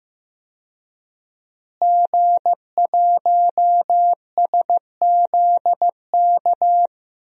Morse code, heard as a keyed tone, about 16 wpm, 700 Hz. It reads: G1SZK
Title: DXCC-100 Call Signs - 5 Characters - Repeated 2x - 15wpm